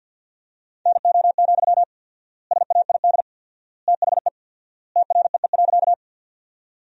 Morse code, heard as a keyed tone, about 50 wpm, 700 Hz. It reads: NO, SAID THE TREE.
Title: 4x Word-Spacing - Sentences from Top 300 Words - 50wpm